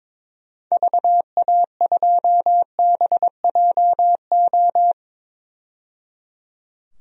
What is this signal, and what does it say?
Morse code, heard as a keyed tone, about 22 wpm, 700 Hz. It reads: VA2BJO
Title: DXCC-100 Call Signs - 6 Characters - 22wpm